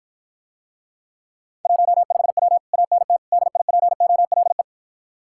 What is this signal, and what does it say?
Morse code, heard as a keyed tone, about 54 wpm, 700 Hz. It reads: 15W ANT DIPOLE